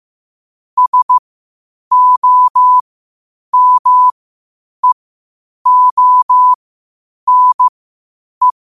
Morse code, heard as a keyed tone, about 15 wpm, spaced as for 8 wpm, 1000 Hz. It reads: SOMEONE
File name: qotdfiles / 8wpm